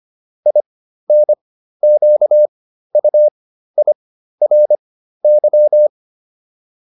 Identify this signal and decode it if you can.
Morse code, heard as a keyed tone, about 25 wpm, spaced as for 13 wpm, 600 Hz. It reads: INQUIRY